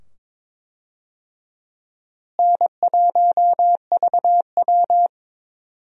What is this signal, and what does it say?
Morse code, heard as a keyed tone, about 22 wpm, 700 Hz. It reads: N1VW
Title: USA Call Signs - 4 Characters - Repeated 2x - 22wpm